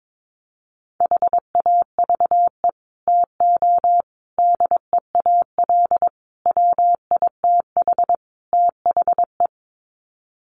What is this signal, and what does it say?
Morse code, heard as a keyed tone, about 22 wpm, 700 Hz. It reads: HAVE TO DEAL WITH THE
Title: Sets of 5 Words Encore2 22wpm